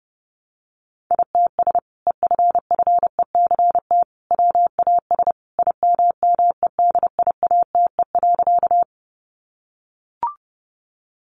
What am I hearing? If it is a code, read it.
Morse code, heard as a keyed tone, about 30 wpm, 700 Hz. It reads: ITS EFFECT WAS IMMEDIATE.
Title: Sentences from Top 1200 Words 30wpm